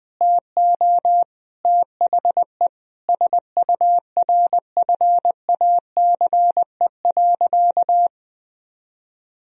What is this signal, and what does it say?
Morse code, heard as a keyed tone, about 20 wpm, 700 Hz. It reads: TO THE SURFACE.
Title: Sentences from Top 1000 Words - Review Part 3 - 20wpm